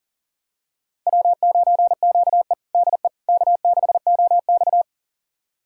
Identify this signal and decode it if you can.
Morse code, heard as a keyed tone, about 37 wpm, 700 Hz. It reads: W9QE DE K6OX
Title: USA Callsign DE Callsign - 4 Characters - 40wpm